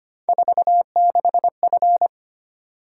Morse code, heard as a keyed tone, about 25 wpm, 700 Hz. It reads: 46F
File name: Three Letter-Number Combinations - 25wpm